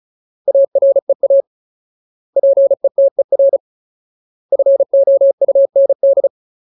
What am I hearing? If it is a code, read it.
Morse code, heard as a keyed tone, about 34 wpm, 550 Hz. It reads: AREA PETER FOUND
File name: Ultimate Word List / 35WPM Ultimate